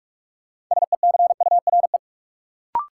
Morse code, heard as a keyed tone, about 46 wpm, 700 Hz. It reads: SECURE
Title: Sentences from Top 1200 Words - New Word Drill Alt - 45wpm